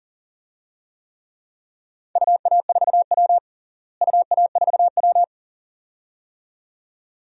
Morse code, heard as a keyed tone, about 37 wpm, 700 Hz. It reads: UA4W UA4W
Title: DXCC-100 Call Signs - 4 Characters - Repeated 2x - 40wpm